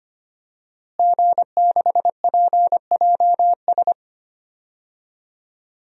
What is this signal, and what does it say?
Morse code, heard as a keyed tone, about 25 wpm, 700 Hz. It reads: G6PJS
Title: DXCC-100 Call Signs - 5 Characters - Repeated 2x - 25wpm